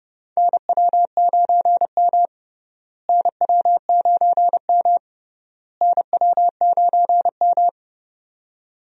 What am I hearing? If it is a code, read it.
Morse code, heard as a keyed tone, about 29 wpm, 700 Hz. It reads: NW9M NW9M NW9M
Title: USA Call Signs - 4 Characters - Repeated 3x - 30wpm